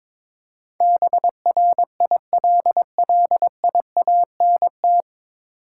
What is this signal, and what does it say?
Morse code, heard as a keyed tone, about 22 wpm, 700 Hz. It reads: BRILLIANT
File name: 9 Letter Words - First 3 Letters Given - 22wpm